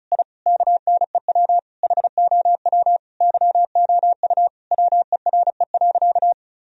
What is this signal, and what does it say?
Morse code, heard as a keyed tone, about 36 wpm, 700 Hz. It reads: I KNEW HOW YOU WERE.
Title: Sentences from Top 1000 Words - Review Part 1 - 35wpm